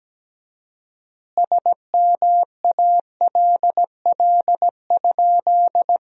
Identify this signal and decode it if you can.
Morse code, heard as a keyed tone, about 17 wpm, 700 Hz. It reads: SMALL?
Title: 4x Word-Spacing - Sentences from Top 200 Words - 17wpm